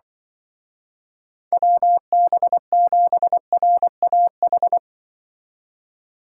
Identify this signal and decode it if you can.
Morse code, heard as a keyed tone, about 25 wpm, 700 Hz. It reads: WB7RAH
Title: Speed Racing - USA 2x3 Call Signs 20wpm